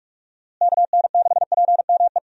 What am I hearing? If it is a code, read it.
Morse code, heard as a keyed tone, about 41 wpm, 700 Hz. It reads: KN6PME
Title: USA Call Signs - 6 Characters - Repeated 2x - 45wpm